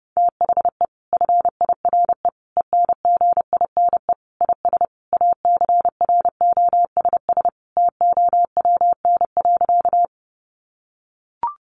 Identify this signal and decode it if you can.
Morse code, heard as a keyed tone, about 30 wpm, 700 Hz. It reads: THE FIRE ENGINE IS ACROSS TOWN.
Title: Sentences from Top 1500 Words 30wpm